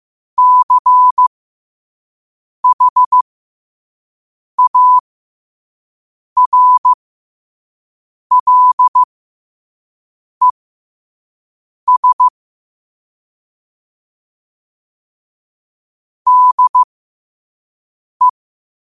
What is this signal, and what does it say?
Morse code, heard as a keyed tone, about 15 wpm, spaced as for 5 wpm, 1000 Hz. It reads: CHARLES DE
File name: qotdfiles / 5wpm